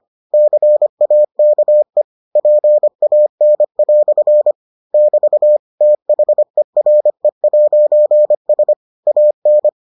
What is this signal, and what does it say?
Morse code, heard as a keyed tone, about 25 wpm, 600 Hz. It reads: CAKE PAN" <BT> THERE'S AN